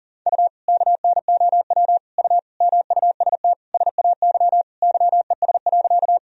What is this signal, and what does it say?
Morse code, heard as a keyed tone, about 40 wpm, 700 Hz. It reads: U KNOW U MUST SAY YES.
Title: Sentences from Top 500 Words and Abbreviations 40wpm